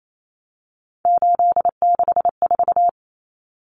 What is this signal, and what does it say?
Morse code, heard as a keyed tone, about 28 wpm, 700 Hz. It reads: 864
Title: Common Cut Numbers - 3-Digit 28wpm